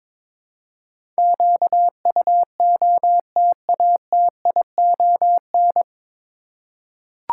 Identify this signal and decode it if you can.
Morse code, heard as a keyed tone, about 22 wpm, 700 Hz. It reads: QUOTATION
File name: 9 Letter Words - First 3 Letters Given - 22wpm